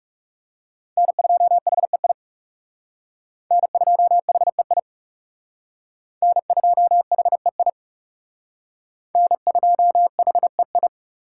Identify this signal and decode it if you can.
Morse code, heard as a keyed tone, about 38 wpm, 700 Hz. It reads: N2HEI N2HEI N2HEI N2HEI